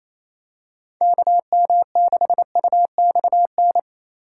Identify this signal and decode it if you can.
Morse code, heard as a keyed tone, about 28 wpm, 700 Hz. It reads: KM6UXN
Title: USA Call Signs - 6 Characters - 2x Character Spacing - 28wpm